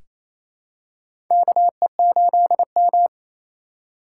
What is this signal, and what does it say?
Morse code, heard as a keyed tone, about 28 wpm, 700 Hz. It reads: KE8M
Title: K1USN SST QSO Practice - 28wpm